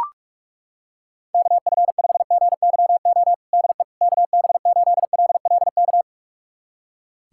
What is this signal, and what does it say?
Morse code, heard as a keyed tone, about 41 wpm, 700 Hz. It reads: KF5GYO DE KB8LLK
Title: USA Callsign DE Callsign 45wpm